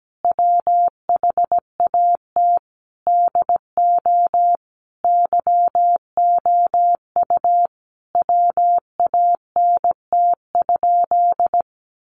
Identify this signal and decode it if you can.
Morse code, heard as a keyed tone, about 17 wpm, 700 Hz. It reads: WHAT DO YOU WANT?
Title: Sentences from Top 1100 Words 17wpm